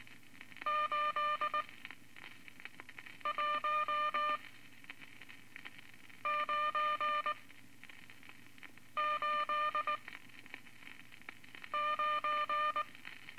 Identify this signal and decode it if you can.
Morse code, heard as a keyed tone, about 21 wpm, spaced as for 5 wpm, 1200 Hz. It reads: 81989
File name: Morse Code Records / 14B LCM 157 Units V Groups of Five